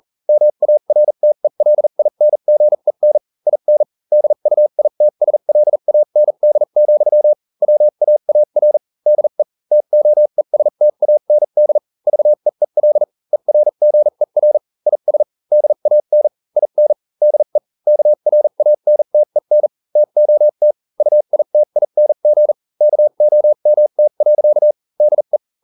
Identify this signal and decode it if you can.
Morse code, heard as a keyed tone, about 40 wpm, spaced as for 35 wpm, 600 Hz. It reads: MARTELINGEN IN DUITSLAND, WAAR DE TOESTAND VEEL ERGER IS DAN IN DE KRANTEN TOT UITING KOMT. DE